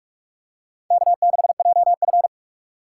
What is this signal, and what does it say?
Morse code, heard as a keyed tone, about 42 wpm, 700 Hz. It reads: K6JF
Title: USA Call Signs - 4 Characters - 2x Character Spacing - 45wpm